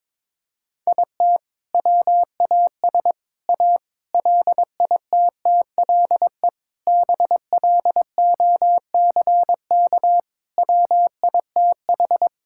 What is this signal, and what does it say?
Morse code, heard as a keyed tone, about 22 wpm, 700 Hz. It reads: IT WAS A LITTLE BLOCK WITH
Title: Sentences from Top 1600 Words 22wpm